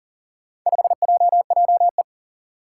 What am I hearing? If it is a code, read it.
Morse code, heard as a keyed tone, about 40 wpm, 700 Hz. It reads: 5JJE
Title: Four Challenging Letter-Number Combinations 40wpm